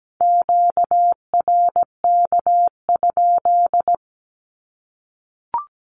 Morse code, heard as a keyed tone, about 17 wpm, 700 Hz. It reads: QRK?
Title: Q-Codes 17wpm